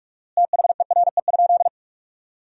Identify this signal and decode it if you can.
Morse code, heard as a keyed tone, about 46 wpm, 700 Hz. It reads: THERE?
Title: 4x Word-Spacing - Sentences from Top 500 Words - 45wpm